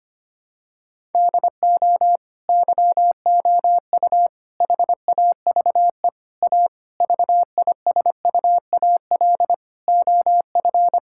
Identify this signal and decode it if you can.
Morse code, heard as a keyed tone, about 25 wpm, 700 Hz. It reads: DO YOU HAVE A VISUAL OF
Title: Sentences from Top 1000 Words - Review Part 6 - 25wpm